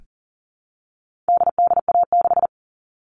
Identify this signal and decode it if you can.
Morse code, heard as a keyed tone, about 40 wpm, 700 Hz. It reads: DDA6
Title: All Cut Numbers - 4-Digit 40wpm